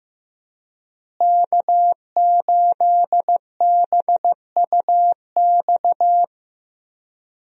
Morse code, heard as a keyed tone, about 15 wpm, 700 Hz. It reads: K8BUX
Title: USA Call Signs - 5 Characters - Repeated 3x - 15wpm